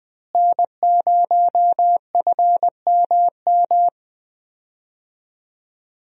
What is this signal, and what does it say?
Morse code, heard as a keyed tone, about 21 wpm, 700 Hz. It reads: N0FMM
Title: USA Call Signs - 5 Characters - Repeated 3x - 20wpm